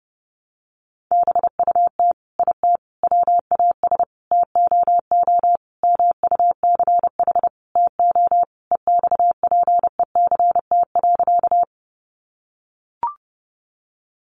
Morse code, heard as a keyed tone, about 30 wpm, 700 Hz. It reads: BUT IT WAS TOO MUCH TO EXPECT.
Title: Sentences from Top 1100 Words 30wpm